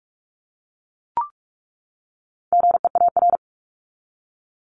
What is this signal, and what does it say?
Morse code, heard as a keyed tone, about 44 wpm, 700 Hz. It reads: GEAR
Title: Top 1251-1500 Words 45wpm